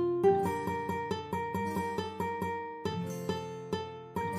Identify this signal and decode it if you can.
There is no code.